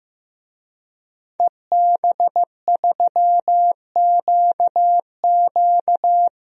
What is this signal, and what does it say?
Morse code, heard as a keyed tone, about 16 wpm, 700 Hz. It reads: EB3QQ
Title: DXCC-100 Call Signs - 5 Characters - Repeated 2x - 15wpm